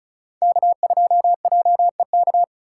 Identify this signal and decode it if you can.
Morse code, heard as a keyed tone, about 36 wpm, 700 Hz. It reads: K2JEK
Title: USA Call Signs - 5 Characters - 3x Character Spacing - 35wpm